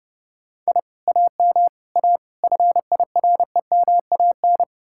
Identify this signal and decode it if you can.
Morse code, heard as a keyed tone, about 39 wpm, 700 Hz. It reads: I AM A FIREMAN